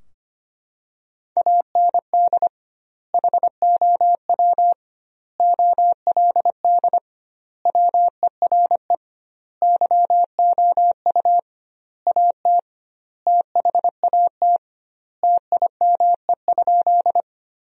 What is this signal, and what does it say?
Morse code, heard as a keyed tone, about 25 wpm, 700 Hz. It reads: AND HOW OLD WERE YOU AT THAT TIME?
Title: 2x Word-Spacing - Sentences from Top 200 Words - 25wpm